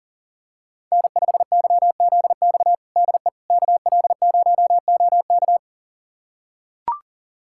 Morse code, heard as a keyed tone, about 40 wpm, 700 Hz. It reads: N5YZX DE KL0OK
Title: USA Callsign DE Callsign - 5 Characters - 40wpm